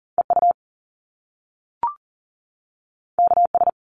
Morse code, heard as a keyed tone, about 41 wpm, 700 Hz. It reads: EU KS
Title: Top Two Letter Combinations in Words 40wpm